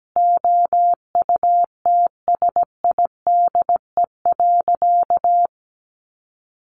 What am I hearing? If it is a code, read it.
Morse code, heard as a keyed tone, about 17 wpm, 700 Hz. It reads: OUTSIDE.
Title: Easy Phrases 17wpm